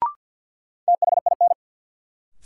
Morse code, heard as a keyed tone, about 52 wpm, 700 Hz. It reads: THIN
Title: Sentences from Top 1200 Words - New Word Drill Alt - 50wpm